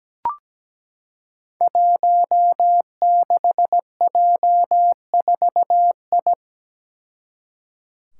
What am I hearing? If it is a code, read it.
Morse code, heard as a keyed tone, about 17 wpm, 700 Hz. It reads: 16J4I
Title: Five Challenging Letter-Number Combinations 17wpm